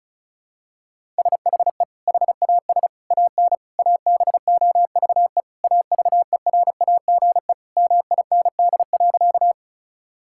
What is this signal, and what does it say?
Morse code, heard as a keyed tone, about 36 wpm, 700 Hz. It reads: SHE HAS AN ABOVE AVERAGE MIND.
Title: Sentences from Top 1000 Words - Review Part 6 - 35wpm